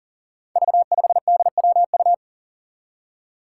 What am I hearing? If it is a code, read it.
Morse code, heard as a keyed tone, about 41 wpm, 700 Hz. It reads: V5DWU
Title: Five Challenging Letter-Number Combinations 40wpm